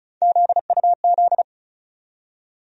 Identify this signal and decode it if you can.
Morse code, heard as a keyed tone, about 36 wpm, 700 Hz. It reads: ZUZ
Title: USA Call Sign Postfixes - 3 Letters 35wpm